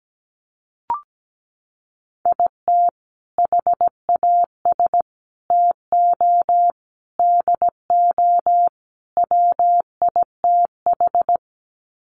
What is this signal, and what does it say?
Morse code, heard as a keyed tone, about 17 wpm, 700 Hz. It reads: IT HAS TO DO WITH
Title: Sets of 5 Words 17wpm